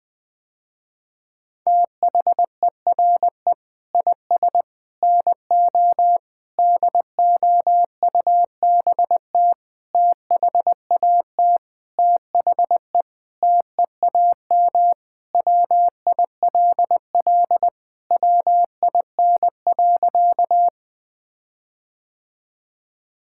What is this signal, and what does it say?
Morse code, heard as a keyed tone, about 20 wpm, 700 Hz. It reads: THERE IS NO DOUBT THAT THE TEAM WILL WIN.
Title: Sentences with 6-Word Phrases - Spoken Beginning - 20wpm